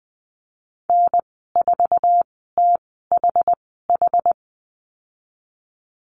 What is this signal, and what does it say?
Morse code, heard as a keyed tone, about 20 wpm, spaced as for 15 wpm, 700 Hz. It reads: N4THH
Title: 062 Lesson 11g - Letter h - Call Signs 20wpm